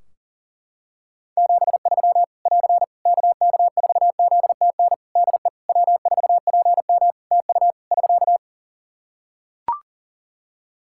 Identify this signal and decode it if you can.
Morse code, heard as a keyed tone, about 40 wpm, 700 Hz. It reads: 73 <AR> KK4ZTN DE W4PM TU <SK>
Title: Practice QSO - Exchange 4 - Ending - 40wpm